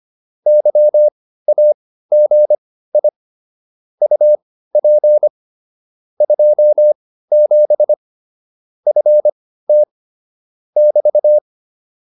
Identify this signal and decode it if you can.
Morse code, heard as a keyed tone, about 25 wpm, spaced as for 15 wpm, 600 Hz. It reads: YAGI UP 27 FT <BT>
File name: QSO307 15